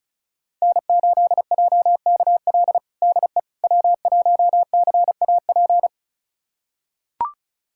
Morse code, heard as a keyed tone, about 36 wpm, 700 Hz. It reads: N8JKL DE W1CAP
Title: USA Callsign DE Callsign - 5 Characters - 35wpm